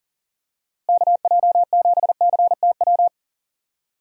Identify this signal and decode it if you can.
Morse code, heard as a keyed tone, about 39 wpm, 700 Hz. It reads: KJ7CTW
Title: USA Call Signs - 6 Characters - Repeated 3x - 40wpm